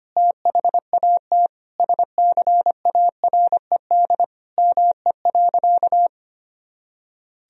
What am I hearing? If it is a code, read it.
Morse code, heard as a keyed tone, about 25 wpm, 700 Hz. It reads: THAT SCARED ME.
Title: Sentences from Top 1000 Words - Review Part 4 - 25wpm